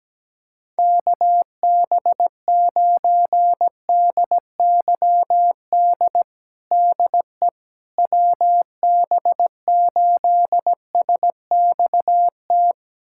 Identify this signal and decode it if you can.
Morse code, heard as a keyed tone, about 17 wpm, 700 Hz. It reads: KB9DYD DE WB8SXT
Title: USA Callsign DE Callsign - 6 Characters - 17wpm